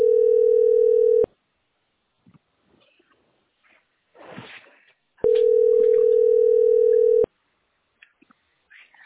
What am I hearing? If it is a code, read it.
Ringback tone (North America), 440 + 480 Hz.